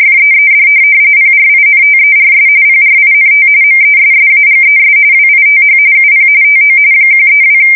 RTTY, about 45.45 baud. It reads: FEED PRINTERS PRINT MESSAGES WITH UP TO 132
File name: fsk170 / teletype-corp